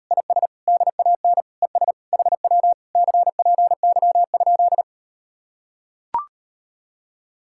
Morse code, heard as a keyed tone, about 38 wpm, 700 Hz. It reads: IS DAN ES HW CPY?